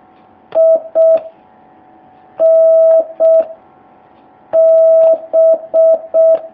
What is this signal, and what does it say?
Morse code, heard as a keyed tone, about 6 wpm, 650 Hz. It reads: I N B
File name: jh8cbh / 2019cw